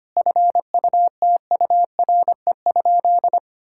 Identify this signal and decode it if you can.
Morse code, heard as a keyed tone, about 25 wpm, 700 Hz. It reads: FUTURE?
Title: Sentences from Top 1000 Words - Review Part 6 - 25wpm